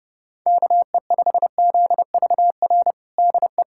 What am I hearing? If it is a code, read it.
Morse code, heard as a keyed tone, about 30 wpm, 700 Hz. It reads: KE5ZVR DE
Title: USA Callsign DE Callsign 30wpm